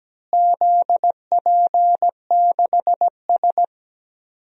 Morse code, heard as a keyed tone, about 17 wpm, 700 Hz. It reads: ZP6S
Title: DXCC-100 Call Signs - 4 Characters - Repeated 3x - 17wpm